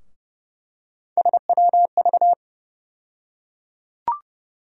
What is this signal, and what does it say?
Morse code, heard as a keyed tone, about 30 wpm, 700 Hz. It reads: SWV
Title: USA Call Sign Postfixes - 3 Letters 30wpm